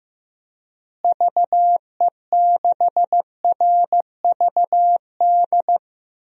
Morse code, heard as a keyed tone, about 15 wpm, 700 Hz. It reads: VE6RVD
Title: DXCC-100 Call Signs - 6 Characters - Repeated 3x - 15wpm